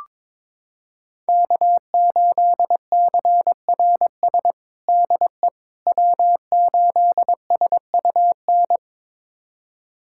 Morse code, heard as a keyed tone, about 22 wpm, 700 Hz. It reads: K8CRS DE W8SUN
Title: USA Callsign DE Callsign - 5 Characters - 22wpm